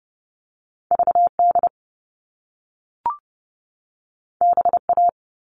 Morse code, heard as a keyed tone, about 30 wpm, 700 Hz. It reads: V D B A
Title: All Cut Numbers - 2-Digit 30wpm